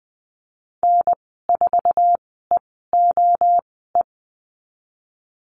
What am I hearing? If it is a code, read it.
Morse code, heard as a keyed tone, about 20 wpm, spaced as for 15 wpm, 700 Hz. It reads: N4EOE
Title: 076 Lesson 13g - Letter L - Call Signs 20wpm